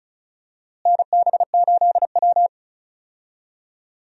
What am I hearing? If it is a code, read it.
Morse code, heard as a keyed tone, about 36 wpm, 700 Hz. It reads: NB8W
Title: USA Call Signs - 4 Characters - Repeated 2x - 35wpm